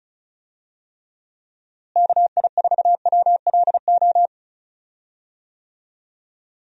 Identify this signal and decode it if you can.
Morse code, heard as a keyed tone, about 36 wpm, 700 Hz. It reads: KI4WLO
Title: USA Call Signs - 6 Characters - Repeated 2x - 35wpm